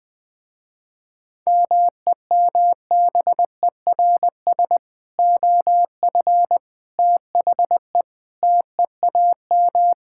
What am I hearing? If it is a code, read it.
Morse code, heard as a keyed tone, about 21 wpm, 700 Hz. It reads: MEMBERS OF THE TEAM
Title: Sentences with 3-Word Phrases - Spoken Beginning - 20wpm